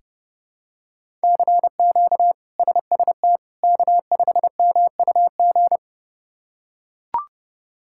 Morse code, heard as a keyed tone, about 32 wpm, 700 Hz. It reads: CQ SST K5MUG
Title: K1USN SST QSO Practice - 30wpm